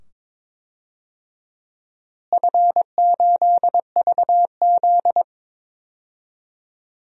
Morse code, heard as a keyed tone, about 22 wpm, 700 Hz. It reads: F8VZ
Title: DXCC-100 Call Signs - 4 Characters - Repeated 2x - 22wpm